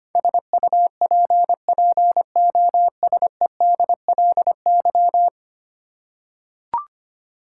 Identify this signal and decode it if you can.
Morse code, heard as a keyed tone, about 25 wpm, 700 Hz. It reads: SUPPOSEDLY
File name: 10 Letter Words - First 3 Letters Given - 25wpm